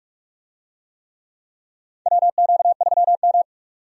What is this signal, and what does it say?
Morse code, heard as a keyed tone, about 47 wpm, 700 Hz. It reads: WQ3M
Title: USA Call Signs - 4 Characters - 45wpm